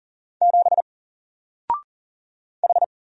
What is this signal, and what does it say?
Morse code, heard as a keyed tone, about 40 wpm, 700 Hz. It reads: 7 H
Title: Single Letter-Number Alt 40wpm